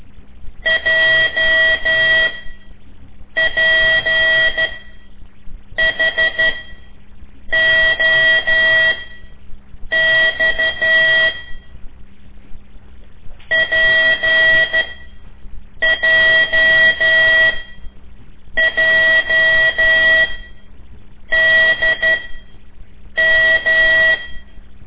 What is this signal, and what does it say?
Morse code, heard as a keyed tone, about 12 wpm, spaced as for 6 wpm, 1950 Hz. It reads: JPHOX PJJDM
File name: ik1weg2 / CORSI CW / lez5-1